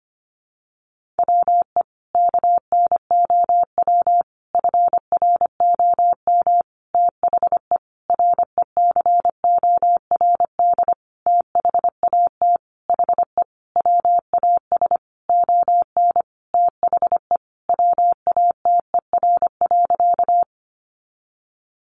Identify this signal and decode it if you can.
Morse code, heard as a keyed tone, about 25 wpm, 700 Hz. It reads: WE KNOW FROM THE RECORD THAT HE WAS ON THE WATER.
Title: Sentences from Top 500 Words 25wpm